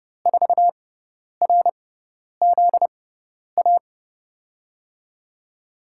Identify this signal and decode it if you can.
Morse code, heard as a keyed tone, about 31 wpm, spaced as for 10 wpm, 700 Hz. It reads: 4RZA
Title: USA Call Signs - 5 Characters - 6x Character Spacing - 30wpm